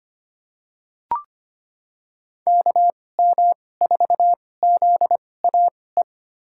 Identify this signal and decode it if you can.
Morse code, heard as a keyed tone, about 25 wpm, spaced as for 18 wpm, 700 Hz. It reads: KM4ZAE
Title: USA Call Signs - 6 Characters - 2x Character Spacing - 25wpm